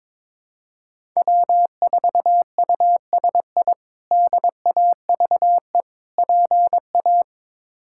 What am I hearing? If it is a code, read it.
Morse code, heard as a keyed tone, about 22 wpm, 700 Hz. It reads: W4USI DAVE PA
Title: K1USN SST QSO Practice - 22wpm